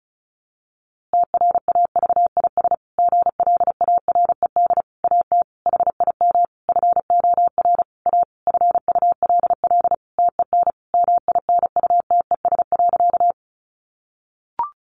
Morse code, heard as a keyed tone, about 34 wpm, 700 Hz. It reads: TRAVIS GLARED AT HIM FOR A FULL TEN MINUTES.
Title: A Sound Of Thunder 35wpm